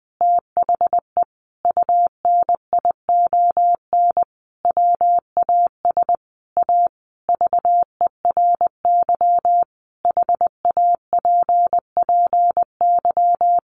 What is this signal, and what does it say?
Morse code, heard as a keyed tone, about 20 wpm, 700 Hz. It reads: THE UNION WAS A VERY HAPPY
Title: Sentences from Top 1400 Words 20wpm